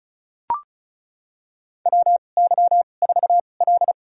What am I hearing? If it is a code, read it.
Morse code, heard as a keyed tone, about 36 wpm, spaced as for 25 wpm, 700 Hz. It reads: WY4L
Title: USA Call Signs - 2x Character Spacing - 35wpm